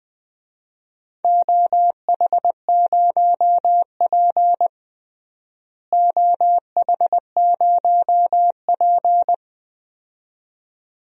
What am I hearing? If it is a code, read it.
Morse code, heard as a keyed tone, about 20 wpm, 700 Hz. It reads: OH0P OH0P
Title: DXCC-100 Call Signs - 4 Characters - Repeated 2x - 20wpm